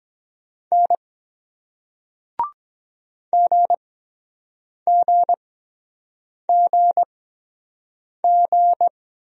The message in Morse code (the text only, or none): N GGGG